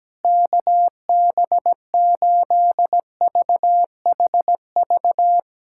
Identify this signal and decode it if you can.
Morse code, heard as a keyed tone, about 17 wpm, 700 Hz. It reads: KB8VHV
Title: USA Callsign DE Callsign - 6 Characters - 17wpm